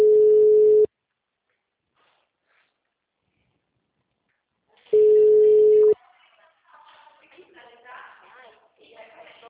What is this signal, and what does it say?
Ringback tone (Europe, 425 Hz), with speech.
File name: out- -2009- - - .43753